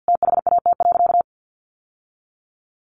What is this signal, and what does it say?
Morse code, heard as a keyed tone, about 50 wpm, 700 Hz. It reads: THAT.
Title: Sentences from Top 1100 Words 50wpm